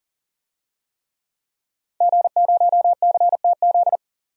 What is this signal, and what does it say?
Morse code, heard as a keyed tone, about 40 wpm, 700 Hz. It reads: G0CTZ
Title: DXCC-100 Call Signs - 5 Characters - 40wpm